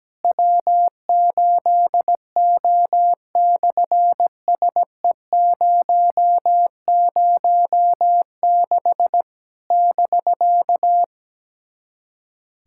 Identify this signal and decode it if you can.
Morse code, heard as a keyed tone, about 17 wpm, 700 Hz. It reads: W8O/SE006 <BK>
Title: SOTA QSO Practice 17wpm